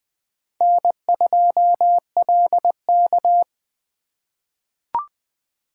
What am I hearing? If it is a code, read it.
Morse code, heard as a keyed tone, about 20 wpm, 700 Hz. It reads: N2LK
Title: K1USN SST QSO Practice - 20wpm